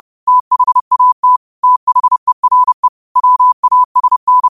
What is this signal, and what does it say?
Morse code, heard as a keyed tone, about 30 wpm, 1000 Hz. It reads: THAT THERE WASN
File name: qotdfiles / 30wpm